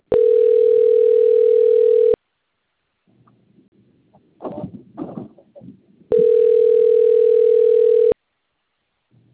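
Ringback tone (North America), 440 + 480 Hz.
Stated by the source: Outgoing call recorded on a phone system labelled Cornelio Procopio.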